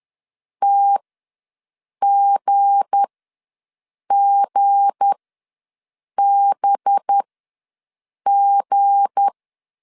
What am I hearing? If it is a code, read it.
Morse code, heard as a keyed tone, about 10 wpm, spaced as for 6 wpm, 800 Hz. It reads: TGGBG
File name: LEZIONI / 7 GF9Z / 7 G ws 10wpm 3843